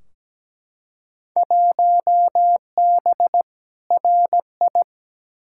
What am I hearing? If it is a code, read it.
Morse code, heard as a keyed tone, about 17 wpm, 700 Hz. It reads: 1B RI
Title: Field Day Exchanges Encore - 17wpm